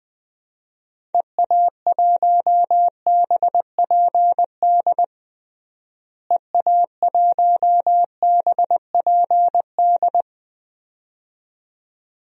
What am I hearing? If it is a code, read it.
Morse code, heard as a keyed tone, about 20 wpm, 700 Hz. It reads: EA1BPD EA1BPD
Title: DXCC-100 Call Signs - 6 Characters - Repeated 2x - 20wpm